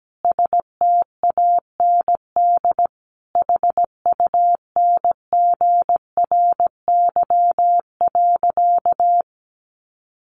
Morse code, heard as a keyed tone, about 17 wpm, 700 Hz. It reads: STAND HUNGRY.
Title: Sentences from Top 1400 Words 17wpm